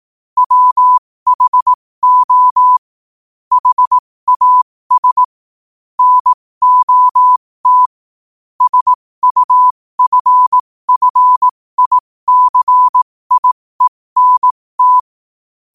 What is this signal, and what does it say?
Morse code, heard as a keyed tone, about 18 wpm, spaced as for 15 wpm, 1000 Hz. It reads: WHO HAS NOT SUFFICIENT
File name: qotdfiles / 15wpm